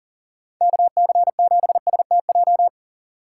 Morse code, heard as a keyed tone, about 40 wpm, 700 Hz. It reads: KC7STJ
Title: USA Call Signs - 6 Characters - 2x Character Spacing - 40wpm